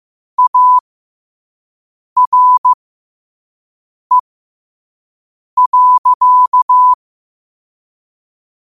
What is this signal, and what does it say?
Morse code, heard as a keyed tone, about 15 wpm, spaced as for 5 wpm, 1000 Hz. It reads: ARE.
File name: qotdfiles / 5wpm